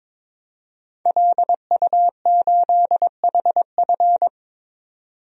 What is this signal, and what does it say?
Morse code, heard as a keyed tone, about 21 wpm, 700 Hz. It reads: LU8HF
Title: DXCC-100 Call Signs - 5 Characters - 22wpm